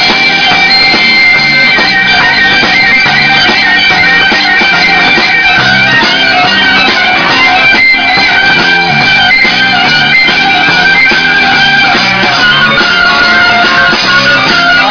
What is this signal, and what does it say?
Dial-up modem handshake.